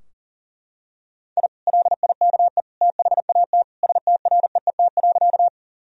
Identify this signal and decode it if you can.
Morse code, heard as a keyed tone, about 40 wpm, 700 Hz. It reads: I LIKE THAT STREET.
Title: Sentences from Top 1000 Words - Review Part 3 - 40wpm